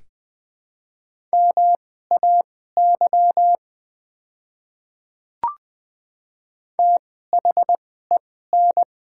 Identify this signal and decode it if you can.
Morse code, heard as a keyed tone, about 20 wpm, spaced as for 14 wpm, 700 Hz. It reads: MAY THEN
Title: 2x Character Spacing - Top 100 Words - 20wpm